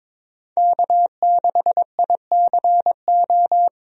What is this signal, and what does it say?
Morse code, heard as a keyed tone, about 22 wpm, 700 Hz. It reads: K6ICO
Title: USA Call Signs - 5 Characters - 22wpm